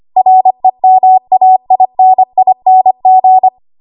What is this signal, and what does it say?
Morse code, heard as a keyed tone, about 25 wpm, 750 Hz. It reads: REMAINING
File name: Archive / 25 WPM / 25WPM